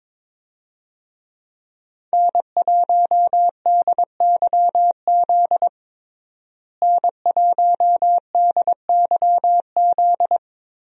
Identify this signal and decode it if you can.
Morse code, heard as a keyed tone, about 22 wpm, 700 Hz. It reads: N1DYZ N1DYZ
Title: USA Call Signs - 5 Characters - Repeated 3x - 22wpm